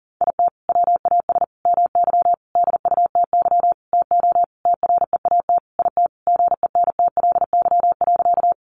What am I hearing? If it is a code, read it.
Morse code, heard as a keyed tone, about 37 wpm, 700 Hz. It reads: IT WAS MY DUTY TO TREAT IT GENTLY.
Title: Sentences from Top 1500 Words 40wpm